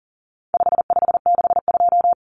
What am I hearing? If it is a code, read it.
Morse code, heard as a keyed tone, about 38 wpm, 700 Hz. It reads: 5562
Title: Common Cut Numbers - 4-Digit 40wpm